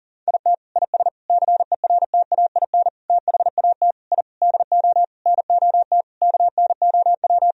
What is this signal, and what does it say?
Morse code, heard as a keyed tone, about 40 wpm, 700 Hz. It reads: IT IS CERTAIN THAT I DO NOT KNOW
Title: Sentences from Top 1000 Words - Review Part 2 - 40wpm